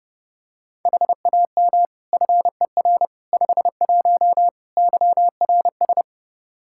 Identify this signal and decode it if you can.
Morse code, heard as a keyed tone, about 32 wpm, 700 Hz. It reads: HAM FER 51 YRS
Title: Practice QSO - Exchange 3 - More Info - 30wpm